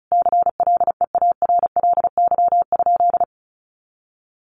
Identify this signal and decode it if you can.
Morse code, heard as a keyed tone, about 34 wpm, 700 Hz. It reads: CLEARLY?
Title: Sentences from Top 1300 Words 35wpm